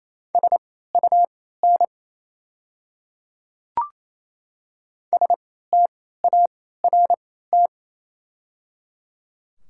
Morse code, heard as a keyed tone, about 28 wpm, spaced as for 16 wpm, 700 Hz. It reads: SUN START
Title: 3x Character Spacing - Top 500 Words - 28wpm